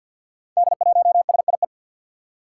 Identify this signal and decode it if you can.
Morse code, heard as a keyed tone, about 47 wpm, 700 Hz. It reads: D1SIE
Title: Five Challenging Letter-Number Combinations 50wpm